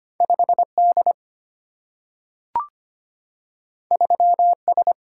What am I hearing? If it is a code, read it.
Morse code, heard as a keyed tone, about 25 wpm, 700 Hz. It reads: 5D 3S
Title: Single-Letter Single-Number 25wpm